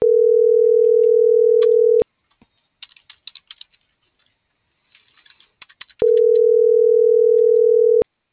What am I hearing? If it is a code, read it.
Ringback tone (North America), 440 + 480 Hz.